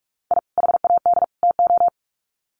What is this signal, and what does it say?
Morse code, heard as a keyed tone, about 44 wpm, 700 Hz. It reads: I HAD TO
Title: Sets of 3 Words 45wpm